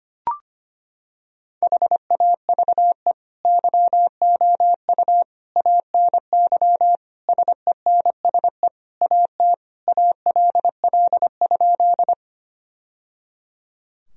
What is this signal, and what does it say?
Morse code, heard as a keyed tone, about 25 wpm, 700 Hz. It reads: HAVE YOU ANY SENSE AT ALL?
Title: Sentences from Top 1000 Words - Review Part 4 - 25wpm